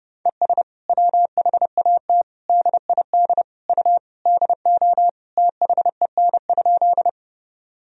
Morse code, heard as a keyed tone, about 30 wpm, 700 Hz. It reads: ES WHAT DID U DO THEN?
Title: Sentences from Top 100 Words and Abbreviations 30wpm